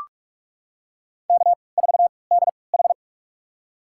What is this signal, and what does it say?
Morse code, heard as a keyed tone, about 44 wpm, spaced as for 25 wpm, 700 Hz. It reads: K4DH
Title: USA Call Signs - 4 Characters - 3x Character Spacing - 45wpm